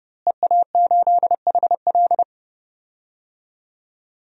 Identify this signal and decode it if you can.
Morse code, heard as a keyed tone, about 31 wpm, 700 Hz. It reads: EA8HL
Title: DXCC-100 Call Signs - 5 Characters - Repeated 2x - 30wpm